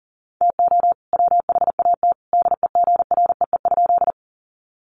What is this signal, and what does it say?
Morse code, heard as a keyed tone, about 40 wpm, 700 Hz. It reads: TO WHAT DEGREE?
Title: Sentences from Top 1500 Words 40wpm